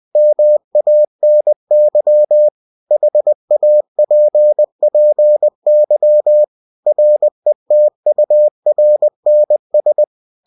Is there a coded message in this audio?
Yes: Morse code from 0.2 to 10.0 s, heard as a keyed tone, about 20 wpm, 600 Hz. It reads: MANY HAPPY RETURNS